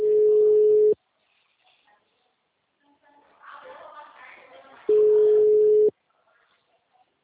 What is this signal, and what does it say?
Ringback tone (Europe, 425 Hz).
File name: out- -2046